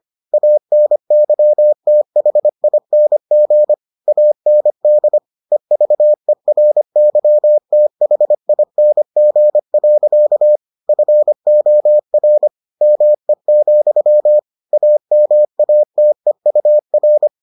Morse code, heard as a keyed tone, about 25 wpm, 600 Hz. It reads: ANYTHING AND EVERYTHING. FOR ME, AMATEUR